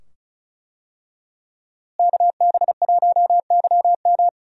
Morse code, heard as a keyed tone, about 35 wpm, 700 Hz. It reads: KB1YM
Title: USA Call Signs - 5 Characters - Repeated 2x - 35wpm